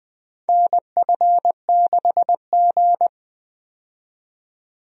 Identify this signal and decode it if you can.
Morse code, heard as a keyed tone, about 20 wpm, 700 Hz. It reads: NF6G
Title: USA Call Signs - 4 Characters - 20wpm